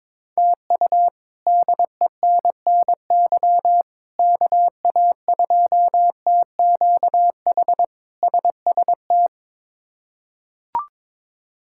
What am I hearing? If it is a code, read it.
Morse code, heard as a keyed tone, about 22 wpm, 700 Hz. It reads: TU DENNY KA2TQH SST
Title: K1USN SST QSO Practice - 22wpm